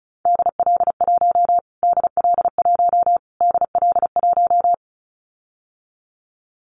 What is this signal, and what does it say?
Morse code, heard as a keyed tone, about 34 wpm, 700 Hz. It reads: DL1 DL1 DL1
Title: Callsign Prefixes for Top 25 DXCC - Repeated 3x - 35wpm